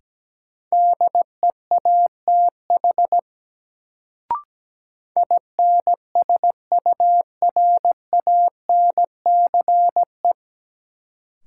Morse code, heard as a keyed tone, about 17 wpm, 700 Hz. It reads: DEATH INSURANCE
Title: Sentences from Top 1100 Words - New Word Drill Alt - 17wpm